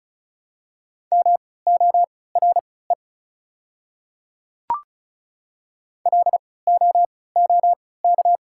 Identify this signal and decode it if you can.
Morse code, heard as a keyed tone, about 36 wpm, spaced as for 20 wpm, 690 Hz. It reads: MORE LOOK
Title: 3x Character Spacing - Top 100 Words - 35wpm